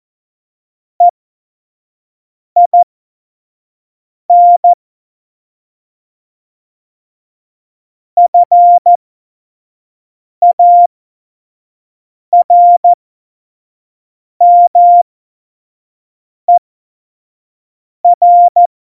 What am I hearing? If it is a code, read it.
Morse code, heard as a keyed tone, about 14 wpm, spaced as for 5 wpm, 700 Hz. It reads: EIN FARMER